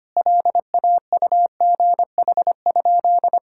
Morse code, heard as a keyed tone, about 25 wpm, 700 Hz. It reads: LAUGH?
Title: Sentences from Top 1000 Words - Review Part 5 - 25wpm